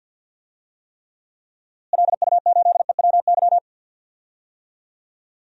Morse code, heard as a keyed tone, about 51 wpm, 700 Hz. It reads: LU8EWY
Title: DXCC-100 Call Signs - 6 Characters - Repeated 2x - 50wpm